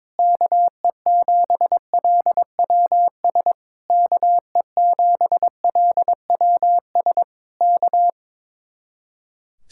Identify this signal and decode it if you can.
Morse code, heard as a keyed tone, about 21 wpm, 700 Hz. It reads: KE7LWS KE7LWS K
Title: Speed-Racing Common QSO Phrases - 22wpm